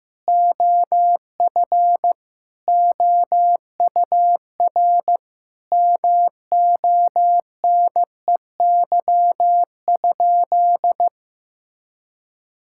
Morse code, heard as a keyed tone, about 15 wpm, 700 Hz. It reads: OF OUR MONEY?
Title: Sentences from Top 1000 Words - Review Part 5 - 15wpm